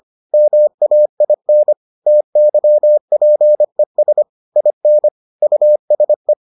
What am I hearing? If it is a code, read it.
Morse code, heard as a keyed tone, about 25 wpm, 600 Hz. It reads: MAIN TYPES IN USE